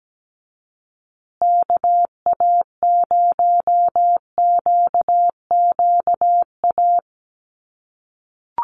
Morse code, heard as a keyed tone, about 17 wpm, 700 Hz. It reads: KA0QQA
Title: USA Call Signs Repeated 3x 17wpm